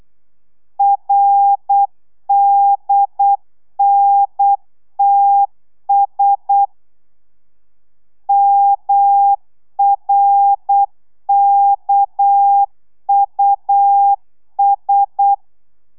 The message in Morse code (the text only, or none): RDNTS MRKUS